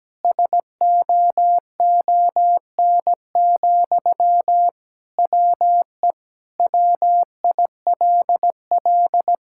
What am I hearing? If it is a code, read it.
Morse code, heard as a keyed tone, about 17 wpm, 700 Hz. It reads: SOON, WE WILL
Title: Sentences from Top 1000 Words - Review Part 6 - 17wpm